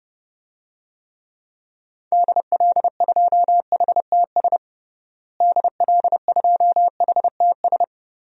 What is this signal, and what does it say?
Morse code, heard as a keyed tone, about 32 wpm, 700 Hz. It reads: DL2HTS DL2HTS
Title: DXCC-100 Call Signs - 6 Characters - Repeated 3x - 30wpm